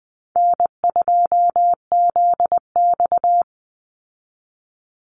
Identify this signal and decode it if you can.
Morse code, heard as a keyed tone, about 20 wpm, 700 Hz. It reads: N2ZX
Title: Easy Call Signs Repeated 2x - 20wpm